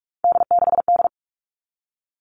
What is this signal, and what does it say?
Morse code, heard as a keyed tone, about 44 wpm, 700 Hz. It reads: D6D
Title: All Cut Numbers - 3-Digit 45wpm